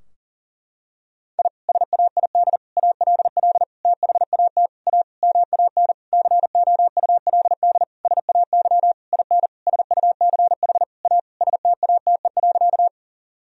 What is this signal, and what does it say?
Morse code, heard as a keyed tone, about 40 wpm, 700 Hz. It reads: I SAID ALL THAT A MAN COULD SAY IN SUCH A STATE.
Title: Sentences from Top 1000 Words - Review Part 2 - 40wpm